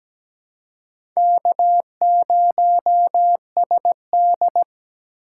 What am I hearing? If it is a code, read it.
Morse code, heard as a keyed tone, about 17 wpm, 700 Hz. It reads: K0SD
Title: USA Call Signs - 4 Characters - 4x Character Spacing - 17wpm